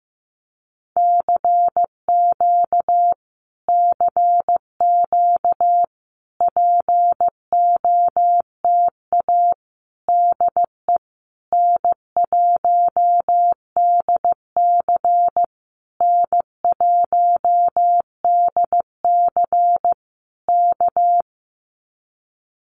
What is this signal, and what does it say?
Morse code, heard as a keyed tone, about 15 wpm, 700 Hz. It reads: CQ CQ POTA DE N1DC N1DC K
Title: Common QSO Phrases Encore 15wpm